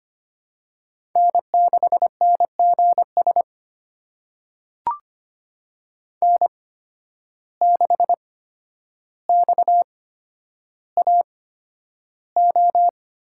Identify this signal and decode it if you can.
Morse code, heard as a keyed tone, about 25 wpm, spaced as for 11 wpm, 700 Hz. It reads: N6NGS N6XAO
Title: USA Call Signs - 5 Characters - 8x Character Spacing - 25wpm